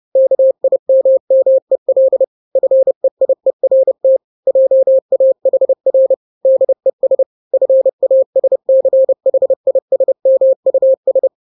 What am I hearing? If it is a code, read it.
Morse code, heard as a keyed tone, about 30 wpm, 530 Hz. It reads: KIMMEL FEIERT JAHR DES FASCHISMUS
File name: daily / 29wpm